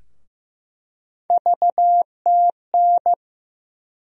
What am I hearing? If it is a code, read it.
Morse code, heard as a keyed tone, about 15 wpm, 700 Hz. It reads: VTN
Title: USA Call Sign Postfixes - 3 Letters 15wpm